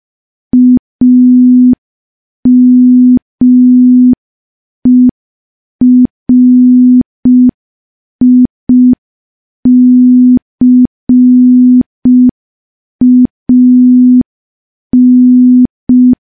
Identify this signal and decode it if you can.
Morse code, heard as a keyed tone, about 5 wpm, 250 Hz. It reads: AMERICAN
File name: morse / morse-training-05WPM